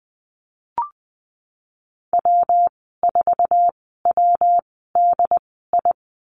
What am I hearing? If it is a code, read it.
Morse code, heard as a keyed tone, about 20 wpm, spaced as for 15 wpm, 700 Hz. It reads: W4WDI
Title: 122 Lesson 20g - Number 3 - Call Signs 20wpm